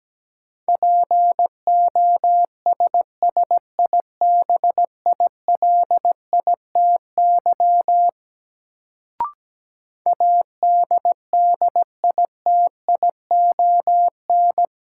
Morse code, heard as a keyed tone, about 17 wpm, 700 Hz. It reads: POSSIBILITY ADDITION
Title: Sets of 3 Words - New Word Drill - 17wpm